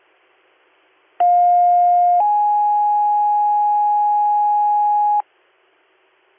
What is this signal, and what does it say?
Two-tone page, heard as a one-second tone, then a three-second tone, about 710 then 830 Hz. It is heard from 1.2 to 5.2 s.